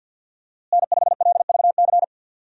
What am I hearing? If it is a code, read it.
Morse code, heard as a keyed tone, about 50 wpm, 700 Hz. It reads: N5LVC
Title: USA Call Signs - 5 Characters - Repeated 2x - 50wpm